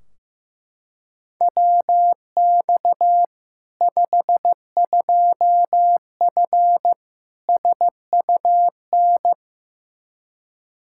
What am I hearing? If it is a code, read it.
Morse code, heard as a keyed tone, about 15 wpm, 700 Hz. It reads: WX 52F SUN
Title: Practice QSO - Exchange 2 - Info - 15wpm